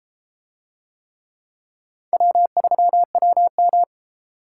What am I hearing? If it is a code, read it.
Morse code, heard as a keyed tone, about 30 wpm, 700 Hz. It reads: W3WM